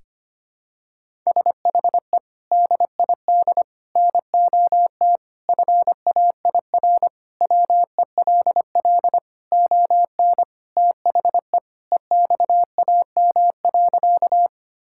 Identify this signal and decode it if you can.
Morse code, heard as a keyed tone, about 25 wpm, 700 Hz. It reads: SHE DID NOT FAIR WELL ON THE EXAM.
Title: Sentences from Top 1000 Words - Review Part 5 - 25wpm